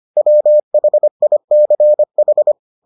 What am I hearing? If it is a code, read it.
Morse code, heard as a keyed tone, about 25 wpm, 600 Hz. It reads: WHICH